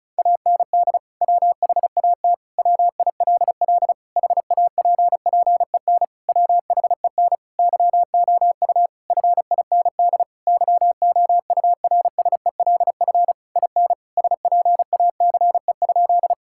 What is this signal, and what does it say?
Morse code, heard as a keyed tone, about 36 wpm, 700 Hz. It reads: AND WHAT WILL HAPPEN WHEN YOU FIND YOURSELF IN SPACE?
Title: Sentences from Top 1000 Words - Review Part 5 - 35wpm